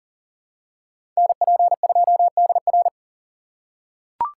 Morse code, heard as a keyed tone, about 40 wpm, 700 Hz. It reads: NP2DR
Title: USA Call Signs - 5 Characters - 4x Character Spacing - 40wpm